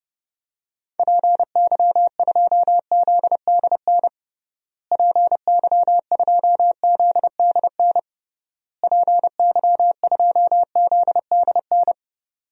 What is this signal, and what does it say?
Morse code, heard as a keyed tone, about 32 wpm, 700 Hz. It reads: PY2ZDN PY2ZDN PY2ZDN
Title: DXCC-100 Call Signs - 6 Characters - Repeated 3x - 30wpm